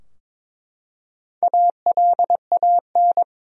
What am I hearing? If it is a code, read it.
Morse code, heard as a keyed tone, about 22 wpm, 700 Hz. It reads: ALAN
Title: USA Common Names - Max 4 Letters - 22wpm